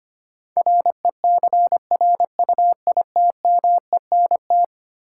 Morse code, heard as a keyed tone, about 25 wpm, 700 Hz. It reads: RECRUITMENT
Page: morse-code-ninja-practice-25wpm